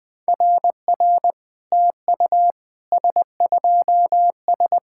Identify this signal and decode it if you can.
Morse code, heard as a keyed tone, about 21 wpm, 690 Hz. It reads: RR TU S2S
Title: SOTA QSO Practice 20wpm